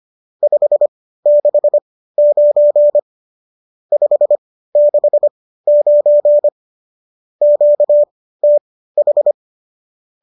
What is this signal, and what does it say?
Morse code, heard as a keyed tone, about 25 wpm, spaced as for 15 wpm, 590 Hz. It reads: 569 569 QTH
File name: qso208 15